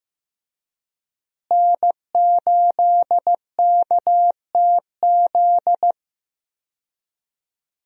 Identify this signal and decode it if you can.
Morse code, heard as a keyed tone, about 15 wpm, 700 Hz. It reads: N8KTZ
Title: USA Call Signs - 5 Characters - Repeated 2x - 15wpm